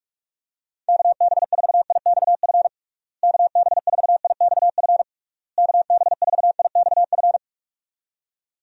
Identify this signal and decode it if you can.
Morse code, heard as a keyed tone, about 46 wpm, 700 Hz. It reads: KB4IXF KB4IXF KB4IXF
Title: USA Call Signs - 6 Characters - Repeated 3x - 45wpm